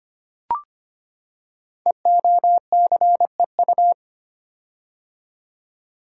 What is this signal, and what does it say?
Morse code, heard as a keyed tone, about 25 wpm, 700 Hz. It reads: EOCEU
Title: Code Groups - 5 Letter:Numbers - 25wpm